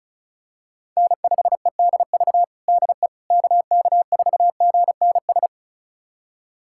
Morse code, heard as a keyed tone, about 36 wpm, 700 Hz. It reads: N5EDV DE KK4GNS
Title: USA Callsign DE Callsign 35wpm